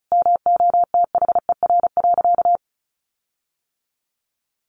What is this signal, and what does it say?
Morse code, heard as a keyed tone, about 34 wpm, 700 Hz. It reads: MOTHER.
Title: Sentences from Top 1400 Words 35wpm